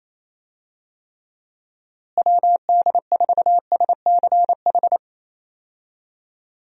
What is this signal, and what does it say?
Morse code, heard as a keyed tone, about 28 wpm, 700 Hz. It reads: WD4SCH
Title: USA Call Signs - 6 Characters - Repeated 2x - 28wpm